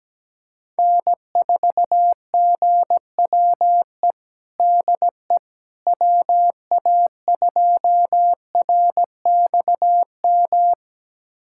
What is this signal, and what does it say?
Morse code, heard as a keyed tone, about 17 wpm, 700 Hz. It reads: N4GWE DE WA2RXM
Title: USA Callsign DE Callsign 17wpm